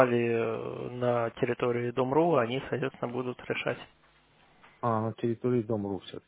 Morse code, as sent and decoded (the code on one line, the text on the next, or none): none